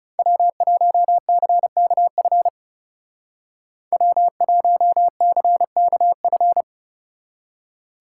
Morse code, heard as a keyed tone, about 37 wpm, 700 Hz. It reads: W1CKF W1CKF
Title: Speed Racing - USA 1x3 Call Signs 30wpm